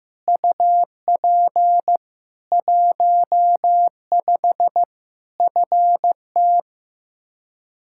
Morse code, heard as a keyed tone, about 15 wpm, 700 Hz. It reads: UP 15 FT
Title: K1USN SST QSO Practice 15wpm